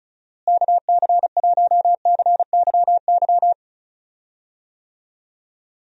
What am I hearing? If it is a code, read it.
Morse code, heard as a keyed tone, about 34 wpm, 700 Hz. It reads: KC1CYY
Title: USA Call Signs - 6 Characters - Repeated 2x - 35wpm